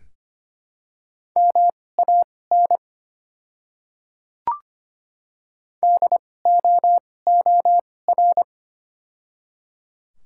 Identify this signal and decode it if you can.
Morse code, heard as a keyed tone, about 25 wpm, spaced as for 18 wpm, 700 Hz. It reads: MAN DOOR
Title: 2x Character Spacing - Top 500 Words - 25wpm